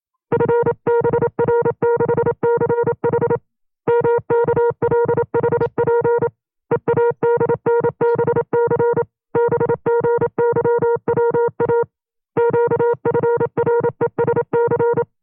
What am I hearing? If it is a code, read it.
Morse code, heard as a keyed tone, about 27 wpm, 470 Hz. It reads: FBR66H MKLHP EADNBC BGYWA QFRESC
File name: FreeFiles / CW-practice / 25WPM-side1